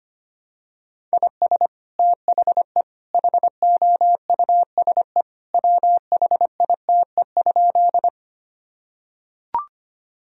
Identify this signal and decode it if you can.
Morse code, heard as a keyed tone, about 25 wpm, 700 Hz. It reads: IS THE HOUSE WHITE?
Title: Sentences from Top 1000 Words - Review Part 1 - 25wpm